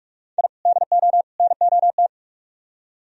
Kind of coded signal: Morse code

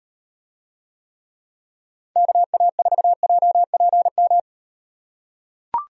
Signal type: Morse code

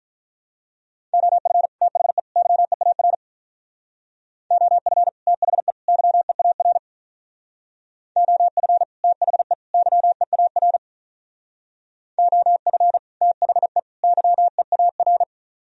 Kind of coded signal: Morse code

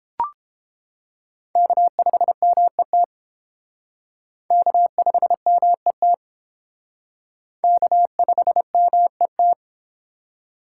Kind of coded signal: Morse code

